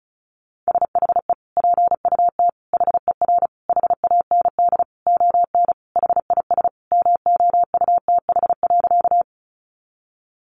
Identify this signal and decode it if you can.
Morse code, heard as a keyed tone, about 34 wpm, 700 Hz. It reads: SHE PUT HER HAND ON HIS MOUTH.
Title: Sentences from Top 1500 Words 35wpm